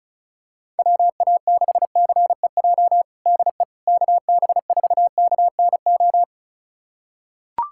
Morse code, heard as a keyed tone, about 34 wpm, 700 Hz. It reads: WA6CEJ DE KB4KNO
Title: USA Callsign DE Callsign - 6 Characters - 35wpm